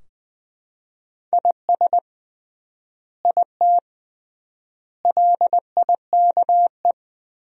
Morse code, heard as a keyed tone, about 19 wpm, 700 Hz. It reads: IS IT LIKE
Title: 3x Word-Spacing - Sentences from Top 100 Words - 20wpm